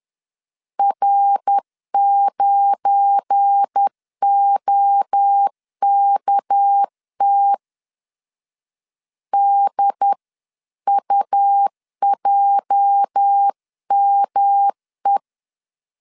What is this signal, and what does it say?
Morse code, heard as a keyed tone, about 10 wpm, 800 Hz. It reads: R9OKT DUJME
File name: LEZIONI / 8 YJ45 / 8 YJ 10wpm 2651